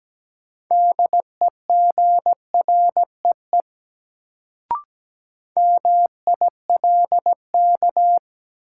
Morse code, heard as a keyed tone, about 17 wpm, 700 Hz. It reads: DEGREE MILK
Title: Sentences from Top 1500 Words - New Word Drill Alt - 17wpm